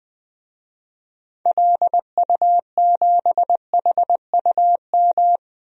Morse code, heard as a keyed tone, about 20 wpm, 700 Hz. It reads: LU7HUM
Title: DXCC-100 Call Signs - 6 Characters - Repeated 2x - 20wpm